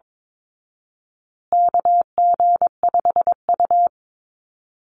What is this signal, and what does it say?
Morse code, heard as a keyed tone, about 22 wpm, 700 Hz. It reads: KG5U
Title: Call Signs Repeated 3x - 22wpm